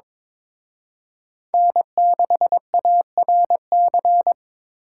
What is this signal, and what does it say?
Morse code, heard as a keyed tone, about 26 wpm, 700 Hz. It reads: N6ARC
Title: Speed Racing - USA 1x3 Call Signs 22wpm